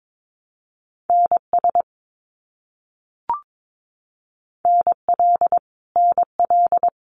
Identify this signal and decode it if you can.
Morse code, heard as a keyed tone, about 22 wpm, 700 Hz. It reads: NS NL NL
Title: Canadian Provinces and Territories - Repeated 3x - 22wpm